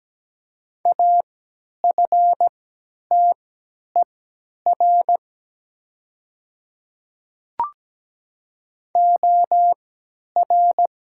Morse code, heard as a keyed tone, about 17 wpm, spaced as for 10 wpm, 700 Hz. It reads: AFTER OR
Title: 3x Character Spacing - Top 200 Words - 17wpm